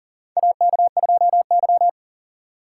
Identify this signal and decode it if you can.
Morse code, heard as a keyed tone, about 40 wpm, 700 Hz. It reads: AK2Y
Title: USA Call Signs - 4 Characters - 2x Character Spacing - 40wpm